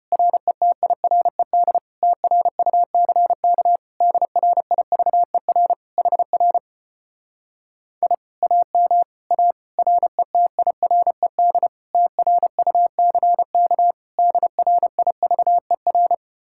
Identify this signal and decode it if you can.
Morse code, heard as a keyed tone, about 33 wpm, 700 Hz. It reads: RETIRED TRUCK DRIVER HR I AM A RETIRED TRUCK DRIVER